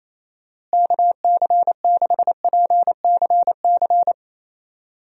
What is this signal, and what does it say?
Morse code, heard as a keyed tone, about 28 wpm, 700 Hz. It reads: KC6PCC
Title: USA Call Signs - 6 Characters - Repeated 3x - 28wpm